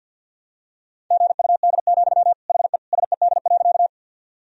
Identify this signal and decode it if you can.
Morse code, heard as a keyed tone, about 51 wpm, 700 Hz. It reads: GUD, HE SED.
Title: Sentences from Top 200 Words and Abbreviations 50wpm